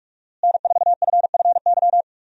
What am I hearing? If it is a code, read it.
Morse code, heard as a keyed tone, about 43 wpm, 700 Hz. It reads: N4FFY
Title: USA Call Signs - 5 Characters - Repeated 3x - 45wpm